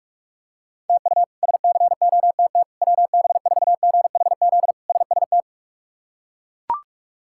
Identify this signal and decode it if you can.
Morse code, heard as a keyed tone, about 41 wpm, 700 Hz. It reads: TU SCOTT WB4GHZ SST
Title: K1USN SST QSO Practice - 45wpm